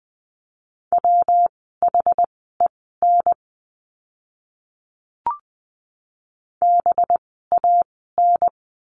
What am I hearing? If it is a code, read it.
Morse code, heard as a keyed tone, about 20 wpm, spaced as for 15 wpm, 700 Hz. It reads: WHEN BAN
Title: 185 Lesson 30f - Letter b - Words 20wpm